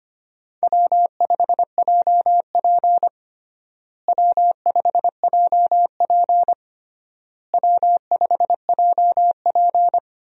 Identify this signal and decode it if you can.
Morse code, heard as a keyed tone, about 25 wpm, 700 Hz. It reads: W5JP W5JP W5JP
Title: USA Call Signs - 4 Characters - Repeated 3x - 25wpm